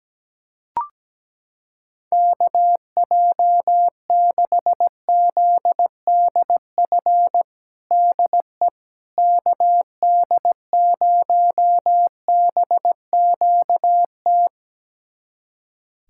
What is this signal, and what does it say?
Morse code, heard as a keyed tone, about 17 wpm, 700 Hz. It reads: KJ6ZDF DE KD0BQT
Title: USA Callsign DE Callsign - 6 Characters - 17wpm